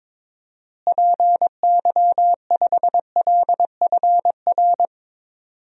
Morse code, heard as a keyed tone, about 22 wpm, 700 Hz. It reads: PY5LFR
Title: DXCC-100 Call Signs - 6 Characters - Repeated 2x - 22wpm